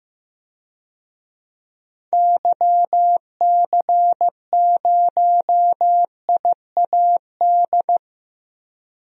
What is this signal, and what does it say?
Morse code, heard as a keyed tone, about 15 wpm, 700 Hz. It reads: YC0IAD
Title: DXCC-100 Call Signs - 6 Characters - Repeated 2x - 15wpm